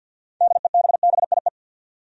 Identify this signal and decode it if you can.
Morse code, heard as a keyed tone, about 53 wpm, 700 Hz. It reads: DEBBIE
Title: USA Common Names - 50wpm